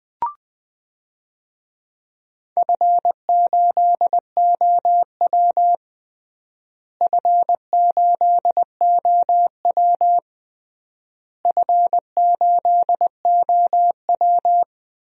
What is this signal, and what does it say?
Morse code, heard as a keyed tone, about 20 wpm, 690 Hz. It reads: F8OW F8OW F8OW
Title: DXCC-100 Call Signs - 4 Characters - Repeated 3x - 20wpm